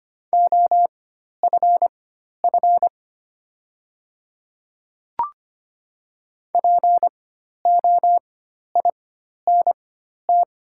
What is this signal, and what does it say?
Morse code, heard as a keyed tone, about 25 wpm, spaced as for 12 wpm, 700 Hz. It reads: OFF POINT
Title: 4x Character Spacing - Top 200 Words - 25wpm